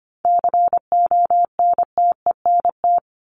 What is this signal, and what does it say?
Morse code, heard as a keyed tone, about 25 wpm, 700 Hz. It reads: CONTENT
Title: Top 751-1000 Words 25wpm